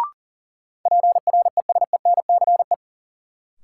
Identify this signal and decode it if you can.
Morse code, heard as a keyed tone, about 40 wpm, 700 Hz. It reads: PRESENCE
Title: Sets of 3 Words - New Word Drill - 40wpm